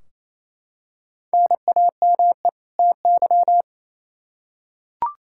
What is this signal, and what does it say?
Morse code, heard as a keyed tone, about 28 wpm, 700 Hz. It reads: NAME TY
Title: Practice QSO - Exchange 1 - Beginning - 28wpm